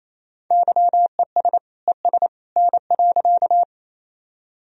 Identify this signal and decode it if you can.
Morse code, heard as a keyed tone, about 28 wpm, 700 Hz. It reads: YES ES N.
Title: Sentences from Top 500 Words and Abbreviations 28wpm